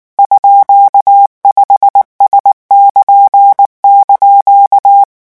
Morse code, heard as a keyed tone, about 19 wpm, 800 Hz. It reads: _5S<KN>)
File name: morse mp3 / revision15-vitesse 18 mots